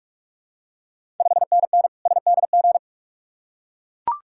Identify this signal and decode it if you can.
Morse code, heard as a keyed tone, about 44 wpm, 690 Hz. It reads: 5NN SDG
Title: Ontario QSO Party Exchange 45wpm